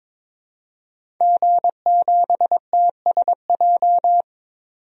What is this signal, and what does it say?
Morse code, heard as a keyed tone, about 22 wpm, 700 Hz. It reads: G7TSJ
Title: DXCC-100 Call Signs - 5 Characters - 22wpm